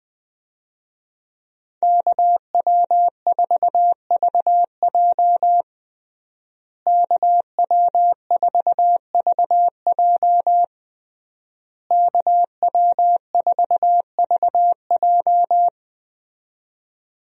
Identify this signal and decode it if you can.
Morse code, heard as a keyed tone, about 20 wpm, 700 Hz. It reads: KW4VJ KW4VJ KW4VJ
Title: USA Call Signs - 5 Characters - Repeated 3x - 20wpm